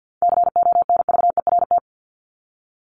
Morse code, heard as a keyed tone, about 50 wpm, 700 Hz. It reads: CONVERT
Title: Top 1751-2000 Words 50wpm